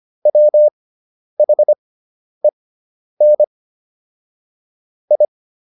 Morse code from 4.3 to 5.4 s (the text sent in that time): I